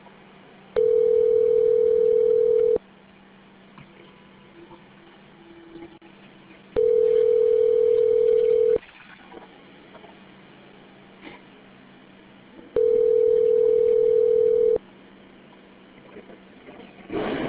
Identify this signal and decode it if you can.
Ringback tone (North America), 440 + 480 Hz.